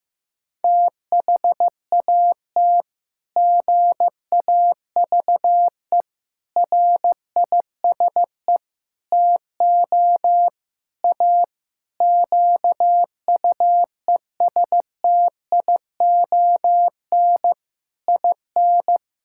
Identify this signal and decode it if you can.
Morse code, heard as a keyed tone, about 15 wpm, 700 Hz. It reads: THAT GAVE RISE TO A QUESTION IN
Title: Sentences from Top 1000 Words - Review Part 6 - 15wpm